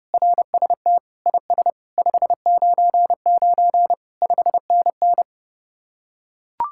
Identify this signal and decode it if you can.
Morse code, heard as a keyed tone, about 30 wpm, 700 Hz. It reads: RST IS 599 5NN